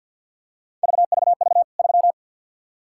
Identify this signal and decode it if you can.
Morse code, heard as a keyed tone, about 48 wpm, 700 Hz. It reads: VVV 3
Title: K1USN SST QSO Practice 50wpm